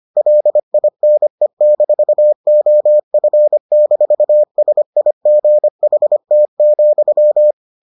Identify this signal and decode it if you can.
Morse code, heard as a keyed tone, about 25 wpm, 600 Hz. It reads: LINE-OF-SIGHT,